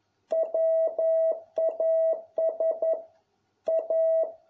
Morse code, heard as a keyed tone, about 11 wpm, 650 Hz. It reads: WAS A